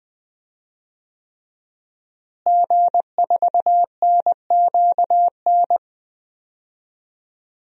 Morse code, heard as a keyed tone, about 20 wpm, 700 Hz. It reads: G4NQN
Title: DXCC-100 Call Signs - 5 Characters - Repeated 3x - 20wpm